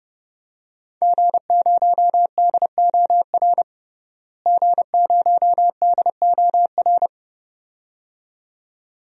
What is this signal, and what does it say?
Morse code, heard as a keyed tone, about 30 wpm, 700 Hz. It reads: G0DOR G0DOR
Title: DXCC-100 Call Signs - 5 Characters - Repeated 2x - 30wpm